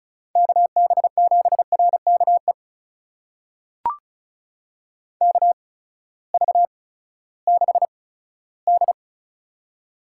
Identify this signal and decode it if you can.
Morse code, heard as a keyed tone, about 36 wpm, spaced as for 18 wpm, 700 Hz. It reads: KB7RKE KV6D
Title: USA Call Signs - 8x Character Spacing - 35wpm